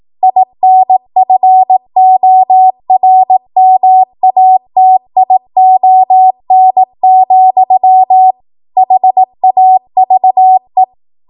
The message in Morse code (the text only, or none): INFORMATION, HAVE